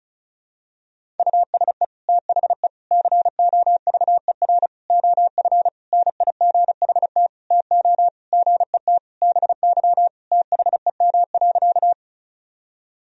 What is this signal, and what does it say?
Morse code, heard as a keyed tone, about 34 wpm, 700 Hz. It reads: USE THE COVER OF NIGHT TO GET BY THEM.
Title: Sentences from Top 1000 Words - Review Part 1 - 35wpm